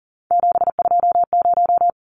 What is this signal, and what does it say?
Morse code, heard as a keyed tone, about 40 wpm, 700 Hz. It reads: 720
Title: Numbers - 3-Digit 40wpm